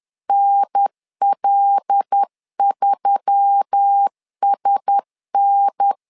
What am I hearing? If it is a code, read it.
Morse code, heard as a keyed tone, about 10 wpm, 800 Hz. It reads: NL3SN
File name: LEZIONI / 5 LBVC / 5 LBV 10wpm 2248